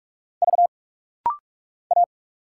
Morse code, heard as a keyed tone, about 46 wpm, 700 Hz. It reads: V A
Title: Single Letter-Number Alt - Rapid-Fire 45wpm